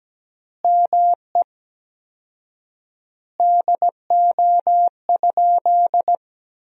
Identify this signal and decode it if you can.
Morse code, heard as a keyed tone, about 17 wpm, 700 Hz. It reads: ME DO?
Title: 4x Word-Spacing - Sentences from Top 200 Words - 17wpm